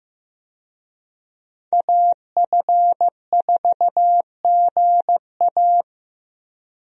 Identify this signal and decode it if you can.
Morse code, heard as a keyed tone, about 15 wpm, 700 Hz. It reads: AF4GA
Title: USA Call Signs - 5 Characters - Repeated 2x - 15wpm